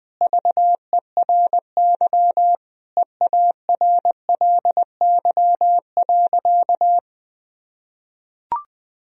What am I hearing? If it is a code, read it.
Morse code, heard as a keyed tone, about 20 wpm, 700 Hz. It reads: VERY EARLY.
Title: Sentences from Top 1000 Words - Review Part 2 - 20wpm